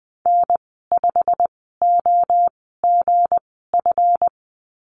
Morse code, heard as a keyed tone, about 20 wpm, spaced as for 14 wpm, 700 Hz. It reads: N5OGF
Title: 193 Lesson 31g - Letter v - Call Signs 20wpm